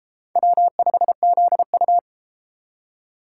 Morse code, heard as a keyed tone, about 33 wpm, 700 Hz. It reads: W5ZU